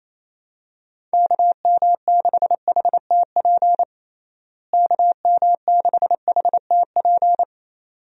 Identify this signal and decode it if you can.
Morse code, heard as a keyed tone, about 28 wpm, 700 Hz. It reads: KM6HTP KM6HTP
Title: USA Call Signs - 6 Characters - Repeated 2x - 28wpm